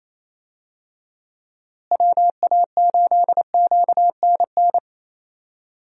Morse code, heard as a keyed tone, about 28 wpm, 700 Hz. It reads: WA8QNN
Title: USA Call Signs - 6 Characters - Repeated 3x - 28wpm